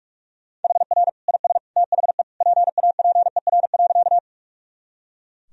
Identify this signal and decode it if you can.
Morse code, heard as a keyed tone, about 41 wpm, 700 Hz. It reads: HR IS THE PAPER.
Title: Sentences from Top 300 Words and Abbreviations 45wpm